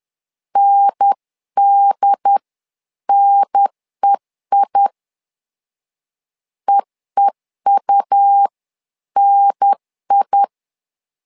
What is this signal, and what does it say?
Morse code, heard as a keyed tone, about 10 wpm, spaced as for 9 wpm, 800 Hz. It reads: NDNEI EEUNI
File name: LEZIONI / 2 UHDN / 2 UHDN 10wpm 1904